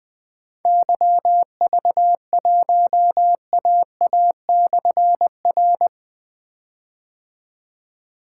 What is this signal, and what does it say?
Morse code, heard as a keyed tone, about 20 wpm, 700 Hz. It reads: YV1AA/R
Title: Difficult Call Signs 20wpm